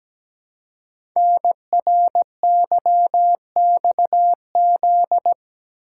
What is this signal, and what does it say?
Morse code, heard as a keyed tone, about 17 wpm, 700 Hz. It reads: NRYXZ
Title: Code Groups - 5 Letters - 17wpm